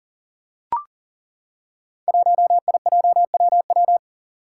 Morse code, heard as a keyed tone, about 40 wpm, 700 Hz. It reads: 1IJWW
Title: Five Challenging Letter-Number Combinations 40wpm